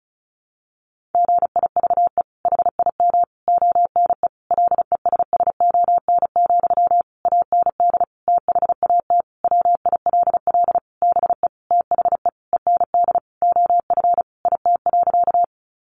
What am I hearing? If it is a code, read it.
Morse code, heard as a keyed tone, about 34 wpm, 700 Hz. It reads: GIVE HIM ONE LESSON, AND THAT WILL BE THE END OF IT.
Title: Sentences from Top 1200 Words 35wpm